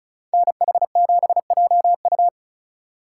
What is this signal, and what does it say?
Morse code, heard as a keyed tone, about 45 wpm, 700 Hz. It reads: NH7JU